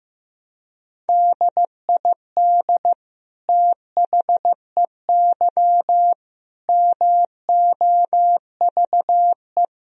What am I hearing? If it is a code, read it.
Morse code, heard as a keyed tone, about 16 wpm, 700 Hz. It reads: DID THEY MOVE
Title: Sentences from Top 1000 Words - Review Part 2 - 15wpm